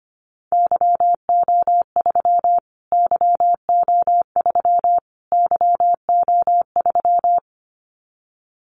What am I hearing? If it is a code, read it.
Morse code, heard as a keyed tone, about 25 wpm, 700 Hz. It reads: YO3 YO3 YO3
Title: Callsign Prefixes for Top 25 DXCC - Repeated 3x - 25wpm